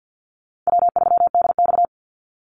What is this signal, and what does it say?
Morse code, heard as a keyed tone, about 50 wpm, 700 Hz. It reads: W3DX
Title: NA SOTA Activator Callsigns 50wpm